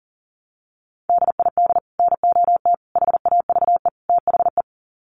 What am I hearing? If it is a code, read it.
Morse code, heard as a keyed tone, about 37 wpm, 700 Hz. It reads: DID NOT HAVE THE
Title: Sets of 4 Words 40wpm